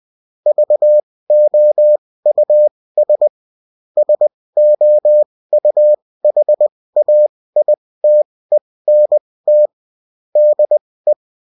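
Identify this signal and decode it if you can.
Morse code, heard as a keyed tone, about 20 wpm, spaced as for 16 wpm, 600 Hz. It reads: VOUS SOUHAITENT DE
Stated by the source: Bulletin in text and CW form.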